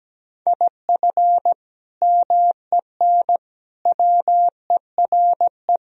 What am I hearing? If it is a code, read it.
Morse code, heard as a keyed tone, about 17 wpm, 700 Hz. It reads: IF MEN WERE
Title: Sentences from Top 1000 Words - Review Part 3 - 17wpm